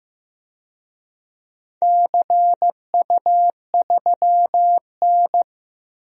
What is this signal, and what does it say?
Morse code, heard as a keyed tone, about 15 wpm, 700 Hz. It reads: CU3N
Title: DXCC-100 Call Signs - 4 Characters - 15wpm